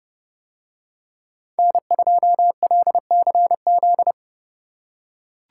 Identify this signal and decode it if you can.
Morse code, heard as a keyed tone, about 29 wpm, 700 Hz. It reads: N2LCZ